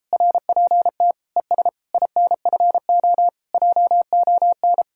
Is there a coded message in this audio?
Yes: Morse code from 0.1 to 4.8 s, heard as a keyed tone, about 33 wpm, 700 Hz. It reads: RPT ES INFO JON